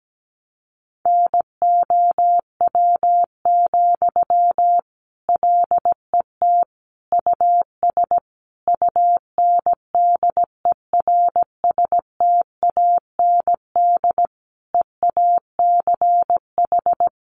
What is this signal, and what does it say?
Morse code, heard as a keyed tone, about 17 wpm, 700 Hz. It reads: NOW, LET US UNDERSTAND EACH
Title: Sentences from Top 1500 Words 17wpm